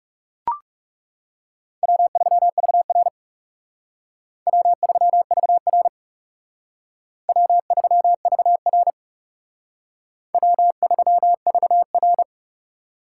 Morse code, heard as a keyed tone, about 38 wpm, 700 Hz. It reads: W3VR W3VR W3VR W3VR